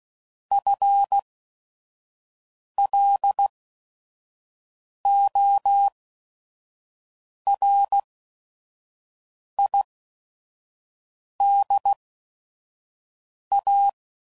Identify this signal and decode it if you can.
Morse code, heard as a keyed tone, about 16 wpm, spaced as for 5 wpm, 800 Hz. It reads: FLORIDA